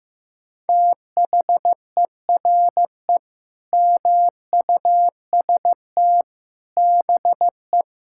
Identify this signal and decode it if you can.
Morse code, heard as a keyed tone, about 15 wpm, 700 Hz. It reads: THERE MUST BE
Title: Sentences from Top 1000 Words - Review Part 6 - 15wpm